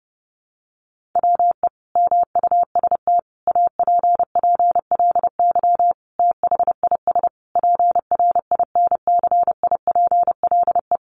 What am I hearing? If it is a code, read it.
Morse code, heard as a keyed tone, about 30 wpm, 700 Hz. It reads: WE MUST APPLY THIS PRINCIPLE
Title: Sentences from Top 1400 Words 30wpm